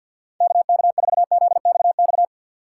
Morse code, heard as a keyed tone, about 52 wpm, 700 Hz. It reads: KK4ZXX
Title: USA Call Signs - 6 Characters - 6x Character Spacing - 50wpm